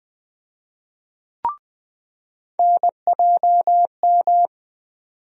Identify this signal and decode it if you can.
Morse code, heard as a keyed tone, about 20 wpm, 700 Hz. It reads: NJM
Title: USA Call Sign Postfixes - 3 Letters 20wpm